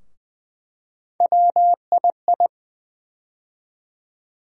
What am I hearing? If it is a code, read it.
Morse code, heard as a keyed tone, about 20 wpm, 700 Hz. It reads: WII
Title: USA Call Sign Postfixes - 3 Letters 20wpm